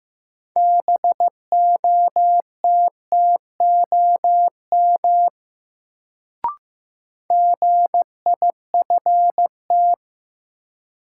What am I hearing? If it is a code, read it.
Morse code, heard as a keyed tone, about 16 wpm, 700 Hz. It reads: BOTTOM GIFT
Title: Sentences from Top 900 Words - New Word Drill - 15wpm